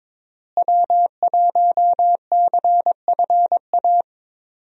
Morse code, heard as a keyed tone, about 21 wpm, 700 Hz. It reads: W1CFA
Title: USA Call Signs - 5 Characters - Repeated 3x - 22wpm